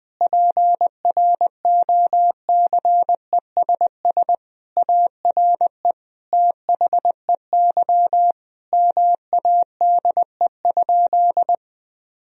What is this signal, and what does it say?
Morse code, heard as a keyed tone, about 21 wpm, 700 Hz. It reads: PROCESS ARE THEY MADE?
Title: Sentences from Top 1000 Words - Review Part 3 - 20wpm